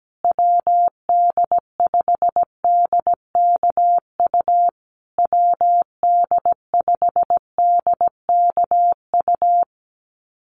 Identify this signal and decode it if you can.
Morse code, heard as a keyed tone, about 17 wpm, 700 Hz. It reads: WD5DKU WD5DKU
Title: USA Call Signs Repeated 3x 17wpm